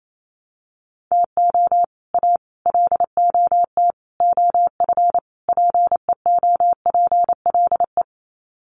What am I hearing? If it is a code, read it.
Morse code, heard as a keyed tone, about 28 wpm, 700 Hz. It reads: TO A LOT OF PEOPLE
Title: Sets of 5 Words Encore 28wpm